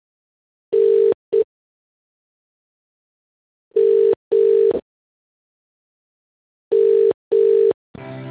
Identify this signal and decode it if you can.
Ringback tone (UK), 400 + 450 Hz.